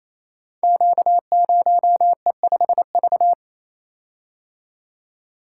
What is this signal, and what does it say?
Morse code, heard as a keyed tone, about 27 wpm, 700 Hz. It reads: Q0E5V
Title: Code Groups - 5 Letter:Numbers - 28wpm